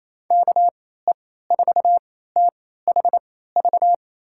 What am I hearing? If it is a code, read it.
Morse code, heard as a keyed tone, about 28 wpm, spaced as for 16 wpm, 700 Hz. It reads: KE4THV
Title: USA Call Signs - 6 Characters - 3x Character Spacing - 28wpm